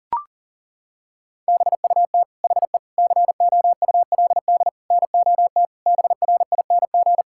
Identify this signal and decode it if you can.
Morse code, heard as a keyed tone, about 40 wpm, 700 Hz. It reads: BUT HE COULD NOT BRING
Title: Sentences from Top 1600 Words 40wpm